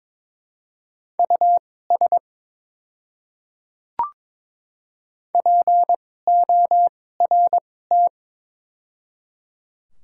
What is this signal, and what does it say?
Morse code, heard as a keyed tone, about 22 wpm, spaced as for 16 wpm, 700 Hz. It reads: US PORT
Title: 2x Character Spacing - Top 200 Words - 22wpm